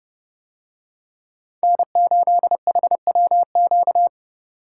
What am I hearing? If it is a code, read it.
Morse code, heard as a keyed tone, about 29 wpm, 700 Hz. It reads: N8HWQ